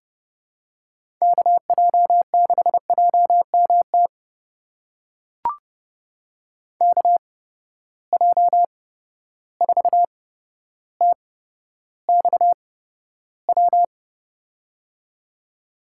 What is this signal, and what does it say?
Morse code, heard as a keyed tone, about 32 wpm, spaced as for 13 wpm, 700 Hz. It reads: KJ6JMT KJ4TXW
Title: USA Call Signs - 6 Characters - 8x Character Spacing - 30wpm